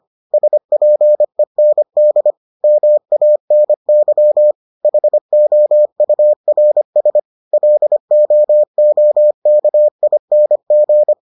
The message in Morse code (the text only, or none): SPEND MANY HOURS LOOKING